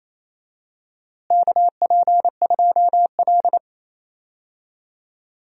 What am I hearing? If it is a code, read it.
Morse code, heard as a keyed tone, about 28 wpm, 700 Hz. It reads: KP2L
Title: DXCC-100 Call Signs - 4 Characters - Repeated 2x - 28wpm